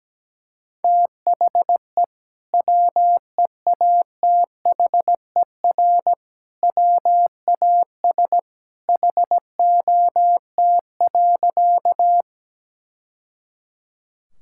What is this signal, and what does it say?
Morse code, heard as a keyed tone, about 17 wpm, 700 Hz. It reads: THE WEATHER WAS HOT.
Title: Sentences from Top 1000 Words - Review Part 5 - 17wpm